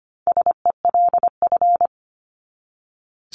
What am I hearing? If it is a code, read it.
Morse code, heard as a keyed tone, about 25 wpm, 700 Hz. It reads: SELF
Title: Top 500 Words 25wpm